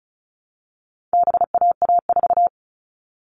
Morse code, heard as a keyed tone, about 34 wpm, 700 Hz. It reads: BAA4
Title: All Cut Numbers - 4-Digit 35wpm